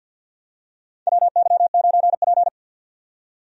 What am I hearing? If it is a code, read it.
Morse code, heard as a keyed tone, about 52 wpm, 700 Hz. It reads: WY9P
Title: USA Call Signs - 4 Characters - 50wpm